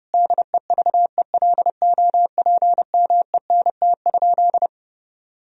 Morse code, heard as a keyed tone, about 30 wpm, 700 Hz. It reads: DEVELOPMENT?
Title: Sentences from Top 1000 Words - Review Part 4 - 30wpm